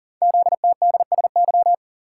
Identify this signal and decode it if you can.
Morse code, heard as a keyed tone, about 40 wpm, 700 Hz. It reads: ZTDSY
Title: Code Groups - 5 Letters - 40wpm